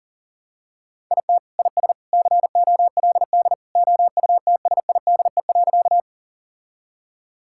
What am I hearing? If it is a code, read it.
Morse code, heard as a keyed tone, about 40 wpm, 700 Hz. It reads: IT IS COLD OUTSIDE.
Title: Sentences from Top 1000 Words - Review Part 6 - 40wpm